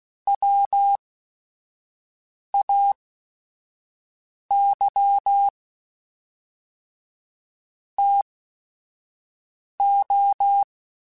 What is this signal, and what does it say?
Morse code, heard as a keyed tone, about 16 wpm, spaced as for 5 wpm, 800 Hz. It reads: WAY TO